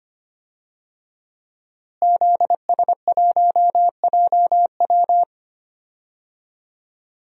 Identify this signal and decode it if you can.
Morse code, heard as a keyed tone, about 25 wpm, 700 Hz. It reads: ZS1JW
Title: DXCC-100 Call Signs - 5 Characters - Repeated 3x - 25wpm